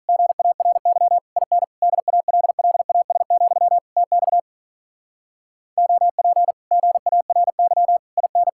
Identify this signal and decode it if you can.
Morse code, heard as a keyed tone, about 45 wpm, 700 Hz. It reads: GARY IN DALLAS, TX OP GARY IN